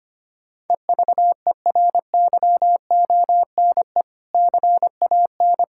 Morse code, heard as a keyed tone, about 25 wpm, 700 Hz. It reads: EVERYONE CAN
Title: Sentences from Top 1600 Words 25wpm